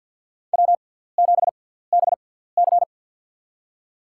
Morse code, heard as a keyed tone, about 53 wpm, spaced as for 17 wpm, 700 Hz. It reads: W7BC
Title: USA Call Signs - 4 Characters - 6x Character Spacing - 50wpm